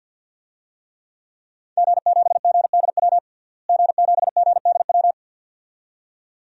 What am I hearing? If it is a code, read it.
Morse code, heard as a keyed tone, about 51 wpm, 700 Hz. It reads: G7GDW G7GDW
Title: DXCC-100 Call Signs - 5 Characters - Repeated 2x - 50wpm